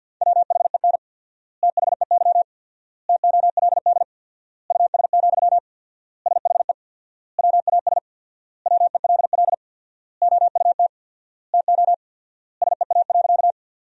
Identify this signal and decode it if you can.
Morse code, heard as a keyed tone, about 51 wpm, 700 Hz. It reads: WHEN THEY TOLD US, SHE WAS WELL OUT TO SEA.
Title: 4x Word-Spacing - Sentences from Top 400 Words - 50wpm